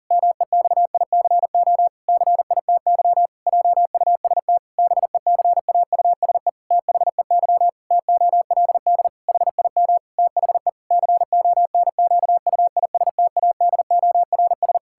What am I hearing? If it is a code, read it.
Morse code, heard as a keyed tone, about 40 wpm, 700 Hz. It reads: MEXICO CITY JUST BECAUSE THEY TOLD HIM THE CONQUISTADORS
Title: Pangram Challenge v2 - 40wpm